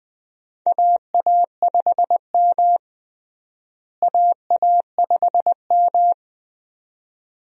Morse code, heard as a keyed tone, about 20 wpm, 700 Hz. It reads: AA5M AA5M
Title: USA Call Signs - 4 Characters - Repeated 2x - 20wpm